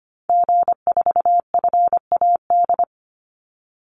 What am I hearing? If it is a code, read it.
Morse code, heard as a keyed tone, about 25 wpm, 700 Hz. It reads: G4FAD
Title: Call Signs - 25wpm